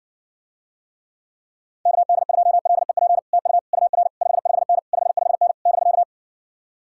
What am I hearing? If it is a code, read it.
Morse code, heard as a keyed tone, about 69 wpm, 700 Hz. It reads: KD2LEP TU UR 55N 55N <BK>